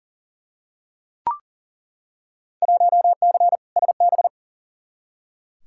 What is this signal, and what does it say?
Morse code, heard as a keyed tone, about 37 wpm, 700 Hz. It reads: 1C SB
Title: Field Day Exchanges Encore - 40wpm